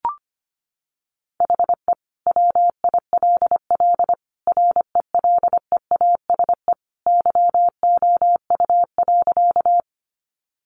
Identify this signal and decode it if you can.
Morse code, heard as a keyed tone, about 25 wpm, 700 Hz. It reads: HE WILL RELEASE YOU.
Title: Sentences from Top 1100 Words 25wpm